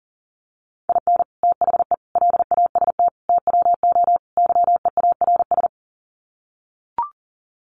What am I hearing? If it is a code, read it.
Morse code, heard as a keyed tone, about 40 wpm, 700 Hz. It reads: IN THE LAST TWO YEARS
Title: Sets of 5 Words 40wpm